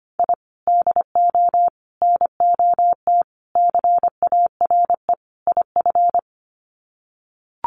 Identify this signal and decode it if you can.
Morse code, heard as a keyed tone, about 25 wpm, 700 Hz. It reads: I DO NOT CARE IF
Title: Sets of 5 Words 25wpm